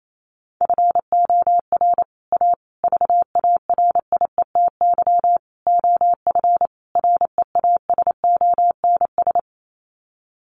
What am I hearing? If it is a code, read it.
Morse code, heard as a keyed tone, about 28 wpm, 700 Hz. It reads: FOR A VARIETY OF REASONS
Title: Sets of 5 Words Encore 28wpm